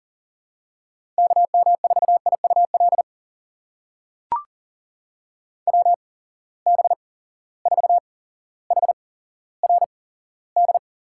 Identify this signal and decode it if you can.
Morse code, heard as a keyed tone, about 37 wpm, spaced as for 17 wpm, 700 Hz. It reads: KM4IUL WB4HRD
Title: USA Call Signs - 6 Characters - 8x Character Spacing - 40wpm